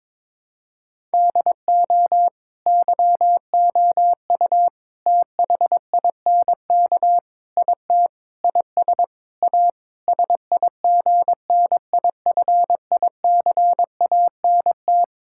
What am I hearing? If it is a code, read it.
Morse code, heard as a keyed tone, about 22 wpm, 700 Hz. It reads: DO YOU THINK IT IS A SIGNIFICANT
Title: Sentences from Top 1000 Words - Review Part 6 - 22wpm